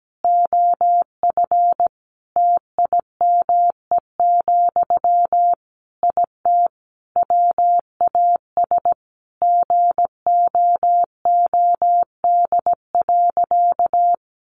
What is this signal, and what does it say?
Morse code, heard as a keyed tone, about 17 wpm, 700 Hz. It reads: OF TIME, IT WAS GOOD.
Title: Sentences from Top 1300 Words 17wpm